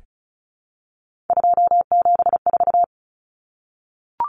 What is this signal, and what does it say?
Morse code, heard as a keyed tone, about 34 wpm, 700 Hz. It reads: 274
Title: Numbers - 3-Digit 35wpm